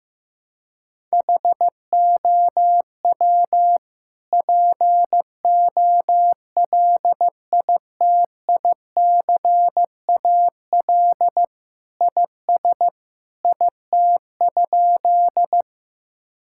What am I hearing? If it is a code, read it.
Morse code, heard as a keyed tone, about 15 wpm, 700 Hz. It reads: HOW POLITICAL IS IT?
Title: Sentences from Top 1000 Words - Review Part 5 - 15wpm